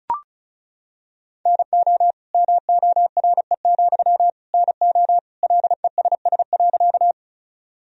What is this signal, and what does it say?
Morse code, heard as a keyed tone, about 34 wpm, 700 Hz. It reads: NO MORE, NO LESS.
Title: Sentences from Top 1000 Words - Review Part 2 - 35wpm